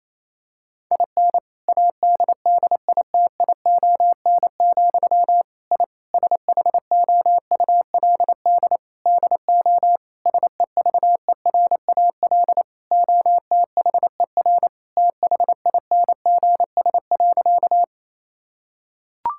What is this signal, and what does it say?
Morse code, heard as a keyed tone, about 28 wpm, 700 Hz. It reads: IN ADDITION, I SHOULD DO SEVERAL OTHER THINGS.
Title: Sentences from Top 1000 Words - Review Part 6 - 28wpm